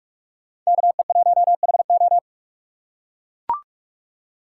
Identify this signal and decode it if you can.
Morse code, heard as a keyed tone, about 41 wpm, 690 Hz. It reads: KE1HO
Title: USA Call Signs - 5 Characters - 6x Character Spacing - 45wpm